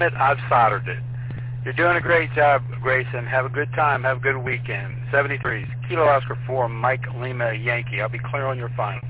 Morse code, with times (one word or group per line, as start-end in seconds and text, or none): none